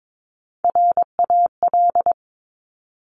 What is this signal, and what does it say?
Morse code, heard as a keyed tone, about 22 wpm, 700 Hz. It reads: RAL
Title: Top Three Letter Combinations in Words 22wpm